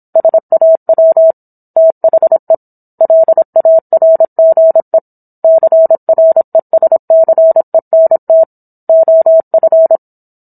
Morse code, heard as a keyed tone, about 26 wpm, 650 Hz. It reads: SAW THE LARGE CRESCENT OF